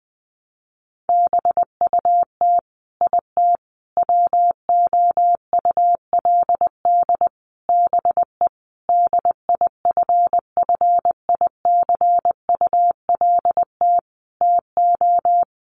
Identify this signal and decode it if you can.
Morse code, heard as a keyed tone, about 20 wpm, 700 Hz. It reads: BUT IT WOULD BE DIFFICULT TO
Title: Sentences from Top 600 Words 20wpm